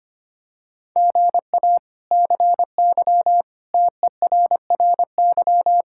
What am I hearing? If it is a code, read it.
Morse code, heard as a keyed tone, about 25 wpm, 700 Hz. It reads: GA CY TERRY
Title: K1USN SST QSO Practice - 25wpm